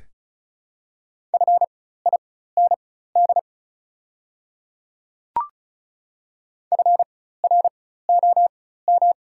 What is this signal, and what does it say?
Morse code, heard as a keyed tone, about 35 wpm, spaced as for 16 wpm, 700 Hz. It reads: FIND FROM
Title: 4x Character Spacing - Top 100 Words - 35wpm